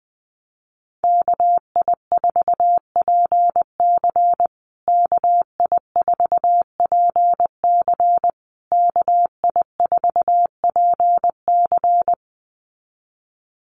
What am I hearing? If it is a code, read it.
Morse code, heard as a keyed tone, about 20 wpm, 700 Hz. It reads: KI4PC KI4PC KI4PC
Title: USA Call Signs Repeated 3x 20wpm